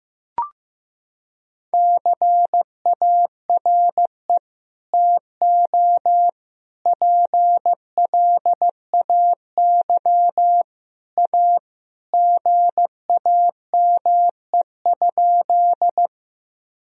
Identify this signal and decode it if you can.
Morse code, heard as a keyed tone, about 15 wpm, 700 Hz. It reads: CARE TO PLAY A GAME?
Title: Sentences from Top 1000 Words - Review Part 1 - 15wpm